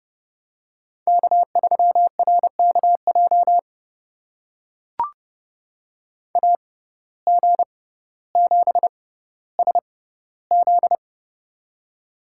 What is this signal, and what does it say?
Morse code, heard as a keyed tone, about 30 wpm, spaced as for 15 wpm, 700 Hz. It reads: K3RKJ AG7SZ
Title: USA Call Signs - 5 Characters - 6x Character Spacing - 30wpm